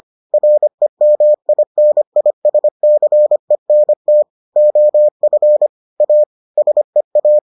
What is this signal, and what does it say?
Morse code, heard as a keyed tone, about 25 wpm, 600 Hz. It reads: REMINISCENT OF A SEA